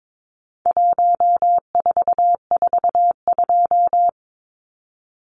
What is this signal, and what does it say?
Morse code, heard as a keyed tone, about 22 wpm, 700 Hz. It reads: 1442
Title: Numbers - 4-Digit 22wpm